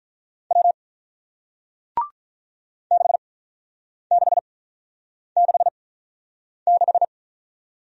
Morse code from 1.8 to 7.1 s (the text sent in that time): 6666